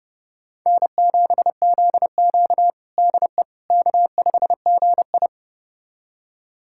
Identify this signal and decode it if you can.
Morse code, heard as a keyed tone, about 30 wpm, 700 Hz. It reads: N7ZQ DE K5GI
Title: USA Callsign DE Callsign - 4 Characters - 30wpm